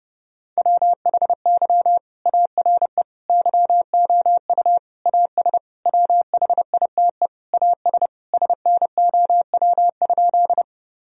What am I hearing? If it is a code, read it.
Morse code, heard as a keyed tone, about 30 wpm, 700 Hz. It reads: WHY ARE YOU AS WHITE AS SNOW?
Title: Sentences from Top 1000 Words - Review Part 3 - 30wpm